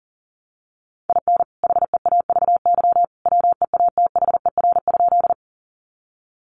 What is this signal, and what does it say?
Morse code, heard as a keyed tone, about 40 wpm, 700 Hz. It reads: IN HEAVY WEATHER?
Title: Sentences from Top 1100 Words 40wpm